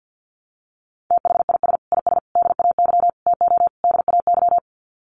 Morse code, heard as a keyed tone, about 50 wpm, 700 Hz. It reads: THIS IS DAY TO DAY
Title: Sets of 5 Words Encore 50wpm